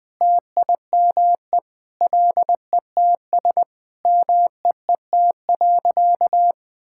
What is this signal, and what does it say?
Morse code, heard as a keyed tone, about 20 wpm, 700 Hz. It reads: TIME LETS MEET.
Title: Sentences with 3-Word Phrases - Spoken Beginning - 20wpm